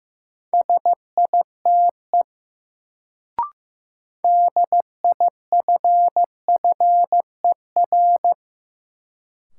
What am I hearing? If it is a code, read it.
Morse code, heard as a keyed tone, about 15 wpm, 700 Hz. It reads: SITE DIFFER
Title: Sentences from Top 1000 Words - Review Part 1 - New Word Drill - 15wpm